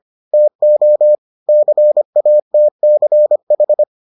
Morse code, heard as a keyed tone, about 25 wpm, 600 Hz. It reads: TO CATCH